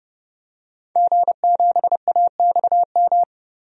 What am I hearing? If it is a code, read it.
Morse code, heard as a keyed tone, about 31 wpm, 700 Hz. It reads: G7AXM
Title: DXCC-100 Call Signs - 5 Characters - Repeated 2x - 30wpm